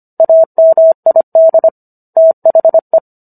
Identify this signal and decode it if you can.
Morse code, heard as a keyed tone, about 25 wpm, 650 Hz. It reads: AMID THE